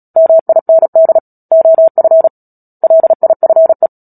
Morse code, heard as a keyed tone, about 37 wpm, 650 Hz. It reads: MIND OF LIFE
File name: Book Two / 10-Chapter 10 - Epilogue - 36 WPM